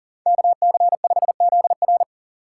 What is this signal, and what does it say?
Morse code, heard as a keyed tone, about 37 wpm, 700 Hz. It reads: KC5ZR
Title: USA Call Signs - 5 Characters - 6x Character Spacing - 40wpm